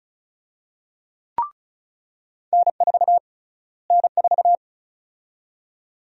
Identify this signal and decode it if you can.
Morse code, heard as a keyed tone, about 36 wpm, 700 Hz. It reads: N4 N4
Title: USA Call Sign Prefixes - Repeated 2x - 35wpm